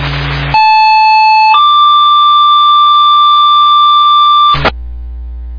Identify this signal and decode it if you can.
Two-tone page, heard as a one-second tone, then a three-second tone, about 870 then 1180 Hz, with noise. It is heard from 0.5 to 4.6 s.
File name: Long Lake Fire